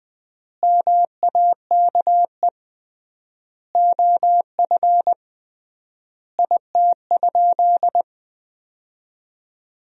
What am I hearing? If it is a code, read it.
Morse code, heard as a keyed tone, about 20 wpm, 700 Hz. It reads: MAKE OF IT?
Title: 3x Word-Spacing - Sentences from Top 100 Words - 20wpm